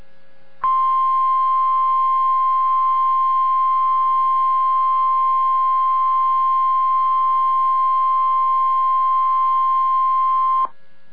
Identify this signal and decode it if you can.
Emergency Alert System (SAME) tones (the 1050 Hz attention tone). It is heard from 0.6 to 10.7 s.